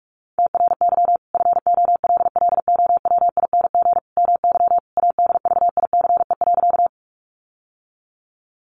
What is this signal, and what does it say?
Morse code, heard as a keyed tone, about 44 wpm, 700 Hz. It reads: TRY FOLLOWING MY ADVICE.
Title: Sentences from Top 1100 Words 45wpm